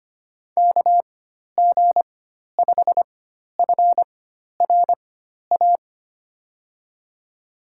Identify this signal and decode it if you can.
Morse code, heard as a keyed tone, about 25 wpm, spaced as for 12 wpm, 700 Hz. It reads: KG5FRA
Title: USA Call Signs - 6 Characters - 4x Character Spacing - 25wpm